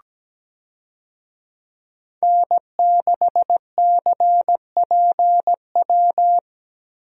Morse code, heard as a keyed tone, about 17 wpm, 700 Hz. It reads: N6CPW
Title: USA Call Signs - 5 Characters - Repeated 2x - 17wpm